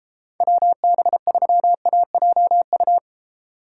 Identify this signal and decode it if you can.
Morse code, heard as a keyed tone, about 32 wpm, 700 Hz. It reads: WB3AJU